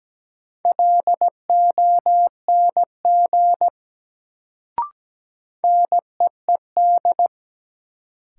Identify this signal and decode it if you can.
Morse code, heard as a keyed tone, about 17 wpm, 700 Hz. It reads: LONG NEED
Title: Sets of 2 Words - New Word Drill - 17wpm